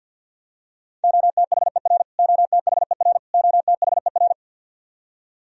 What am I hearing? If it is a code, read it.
Morse code, heard as a keyed tone, about 51 wpm, 700 Hz. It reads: OTHER OTHER OTHER
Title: Top 100 Words Repeated 50wpm